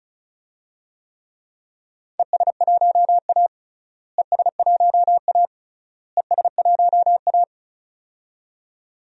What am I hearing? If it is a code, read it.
Morse code, heard as a keyed tone, about 34 wpm, 700 Hz. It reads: ES1A ES1A ES1A
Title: DXCC-100 Call Signs - 4 Characters - Repeated 3x - 35wpm